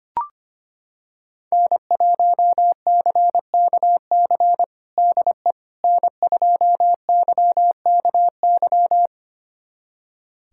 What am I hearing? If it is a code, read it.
Morse code, heard as a keyed tone, about 25 wpm, 700 Hz. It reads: N1CKC DE N2YKY
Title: USA Callsign DE Callsign - 5 Characters - 25wpm